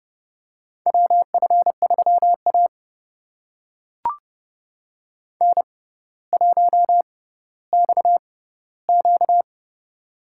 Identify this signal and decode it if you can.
Morse code, heard as a keyed tone, about 30 wpm, spaced as for 15 wpm, 700 Hz. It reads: WF3A N1XQ
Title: USA Call Signs - 4 Characters - 6x Character Spacing - 30wpm